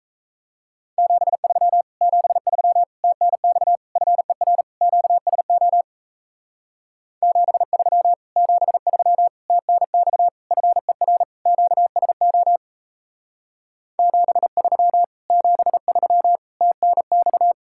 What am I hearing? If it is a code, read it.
Morse code, heard as a keyed tone, about 37 wpm, 700 Hz. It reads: 73 73 TNX FER QSO 73 73 TNX FER QSO 73 73 TNX